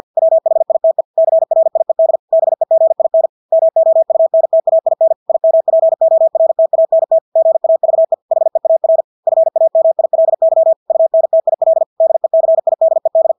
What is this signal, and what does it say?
Morse code, heard as a keyed tone, about 50 wpm, 650 Hz. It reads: WHITE CRIED BEGIN MOUNTAIN IMPORTANT GAVE HEAR FAMILY UNTIL DECIDED
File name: Engelske ord 50 wpm